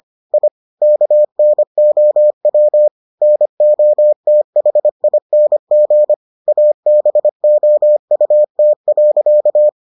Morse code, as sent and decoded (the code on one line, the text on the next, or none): .. / -.- -. --- .-- / -. --- - .... .. -. --. / .- -... --- ..- - .-.-.-
I KNOW NOTHING ABOUT.